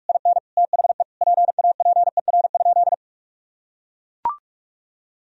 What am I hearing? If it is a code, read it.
Morse code, heard as a keyed tone, about 43 wpm, 700 Hz. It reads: IN THE PAPER?
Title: Sentences from Top 1000 Words - Review Part 2 - 45wpm